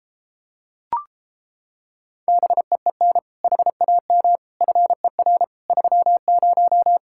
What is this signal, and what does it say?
Morse code, heard as a keyed tone, about 29 wpm, 700 Hz. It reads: BEEN HAM FER 30